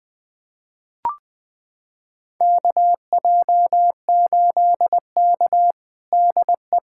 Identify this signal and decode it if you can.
Morse code, heard as a keyed tone, about 20 wpm, 700 Hz. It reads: KJ8K DE
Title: USA Callsign DE Callsign - 4 Characters - 20wpm